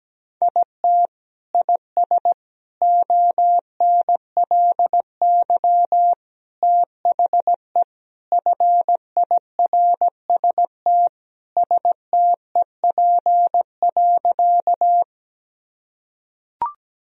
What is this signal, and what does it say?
Morse code, heard as a keyed tone, about 17 wpm, 700 Hz. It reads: IT IS ONLY THE FIRST STEP.
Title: Sentences from Top 1000 Words - Review Part 2 - 17wpm